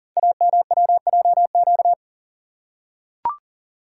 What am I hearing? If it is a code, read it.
Morse code, heard as a keyed tone, about 43 wpm, 700 Hz. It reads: AMWJQ
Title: Code Groups - 5 Letters - 40wpm